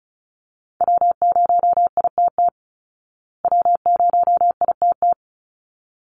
Morse code, heard as a keyed tone, about 34 wpm, 700 Hz. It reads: W0ITT W0ITT
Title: Call Signs Repeated 3x - 35wpm